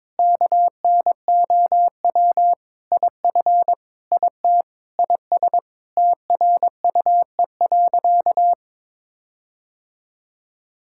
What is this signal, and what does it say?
Morse code, heard as a keyed tone, about 22 wpm, 700 Hz. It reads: KNOW IF IT IS TRUE.
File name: practice / 22wpm / Sentences with 4-Word Phrases - 22wpm